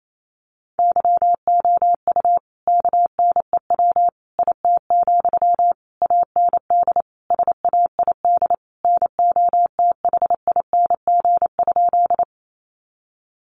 Morse code, heard as a keyed tone, about 28 wpm, 700 Hz. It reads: YOU KNEW IT, AND SAID NOTHING?
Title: Sentences from Top 500 Words 28wpm